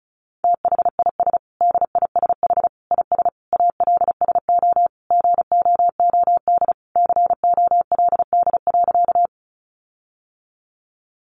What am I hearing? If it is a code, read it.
Morse code, heard as a keyed tone, about 34 wpm, 700 Hz. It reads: THIS DISH IS ALSO GOOD COLD.
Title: Sentences from Top 1300 Words 35wpm